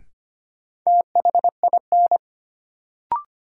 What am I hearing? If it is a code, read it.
Morse code, heard as a keyed tone, about 25 wpm, 700 Hz. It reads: THIN
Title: Sentences from Top 1200 Words - New Word Drill - 25wpm